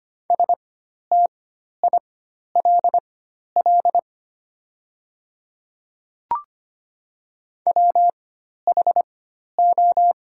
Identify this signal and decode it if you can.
Morse code, heard as a keyed tone, about 25 wpm, spaced as for 12 wpm, 700 Hz. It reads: STILL WHO
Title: 4x Character Spacing - Top 300 Words - 25wpm